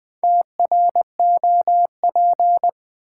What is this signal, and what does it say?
Morse code, heard as a keyed tone, about 20 wpm, 700 Hz. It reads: TROP
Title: Build A Word - 2 Letters At A Time - 8-Letter Words - 20wpm